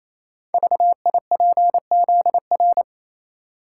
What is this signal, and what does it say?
Morse code, heard as a keyed tone, about 27 wpm, 700 Hz. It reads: VIPZR
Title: Code Groups - 5 Letters - 28wpm